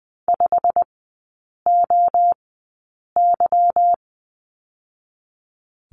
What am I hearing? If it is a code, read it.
Morse code, heard as a keyed tone, about 20 wpm, spaced as for 8 wpm, 700 Hz. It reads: 5OY
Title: 145 Lesson 24e - Letter y - Three Characters 20wpm